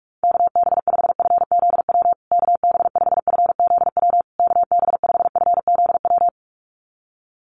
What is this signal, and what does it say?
Morse code, heard as a keyed tone, about 44 wpm, 700 Hz. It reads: KB5FZW KB5FZW KB5FZW
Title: USA Call Signs Repeated 3x 45wpm